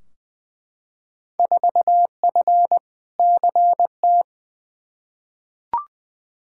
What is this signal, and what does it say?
Morse code, heard as a keyed tone, about 21 wpm, 700 Hz. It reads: 4F CT
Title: Field Day Exchanges Encore - 20wpm